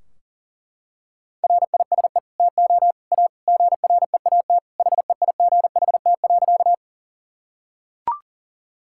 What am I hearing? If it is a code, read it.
Morse code, heard as a keyed tone, about 40 wpm, 700 Hz. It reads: RISE TO A GREAT HEIGHT.
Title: Sentences from Top 1600 Words 40wpm